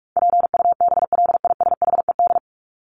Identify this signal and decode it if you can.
Morse code, heard as a keyed tone, about 44 wpm, 700 Hz. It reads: PUBLISHED
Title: ED Suffix - Top 100 Words with Base Words 45wpm